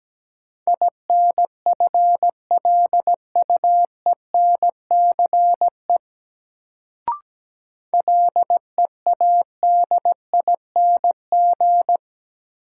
Morse code, heard as a keyed tone, about 17 wpm, 700 Hz. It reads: INFLUENCE LEADING
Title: Sentences from Top 1100 Words - New Word Drill - 17wpm